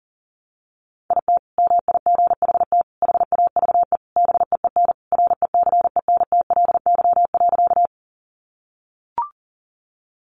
Morse code, heard as a keyed tone, about 40 wpm, 700 Hz. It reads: IT MIGHT HAVE BEEN RECENTLY.
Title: Sentences from Top 1200 Words 40wpm